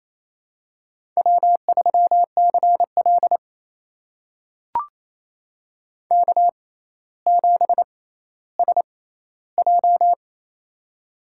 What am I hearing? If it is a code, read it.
Morse code, heard as a keyed tone, about 28 wpm, spaced as for 14 wpm, 700 Hz. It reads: W3CL K7SJ
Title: USA Call Signs - 4 Characters - 6x Character Spacing - 28wpm